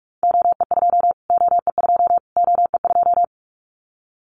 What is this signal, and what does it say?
Morse code, heard as a keyed tone, about 44 wpm, 700 Hz. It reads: OE2 OE2 OE2
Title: Callsign Prefixes for Top 25 DXCC - Repeated 3x - 45wpm